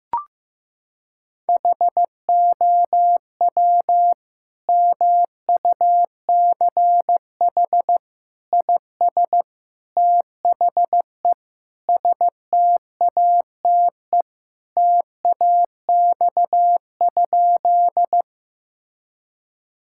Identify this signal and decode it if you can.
Morse code, heard as a keyed tone, about 15 wpm, 700 Hz. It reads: HOW MUCH IS THE STATE TAX?
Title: Sentences from Top 1000 Words - Review Part 5 - 15wpm